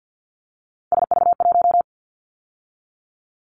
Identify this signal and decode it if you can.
Morse code, heard as a keyed tone, about 50 wpm, 700 Hz. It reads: SV1
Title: Callsign Prefixes for Top 50 DXCC - 50wpm